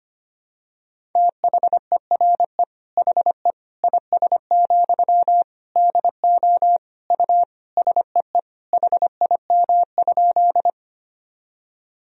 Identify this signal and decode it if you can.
Morse code, heard as a keyed tone, about 25 wpm, 700 Hz. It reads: THERE HE IS, DO U SEE HIM?
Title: Sentences from Top 100 Words and Abbreviations 25wpm